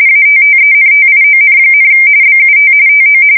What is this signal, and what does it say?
RTTY, about 45.45 baud. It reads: ATTENTION, BUT TO
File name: fsk170 / USS-New-Jersey